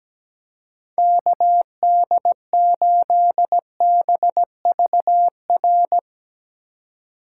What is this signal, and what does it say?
Morse code, heard as a keyed tone, about 20 wpm, 700 Hz. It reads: KD8BVR
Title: Speed Racing - USA 2x3 Call Signs 17wpm